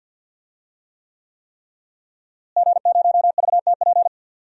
Morse code, heard as a keyed tone, about 52 wpm, 700 Hz. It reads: G0VTP
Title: DXCC-100 Call Signs - 5 Characters - Repeated 3x - 50wpm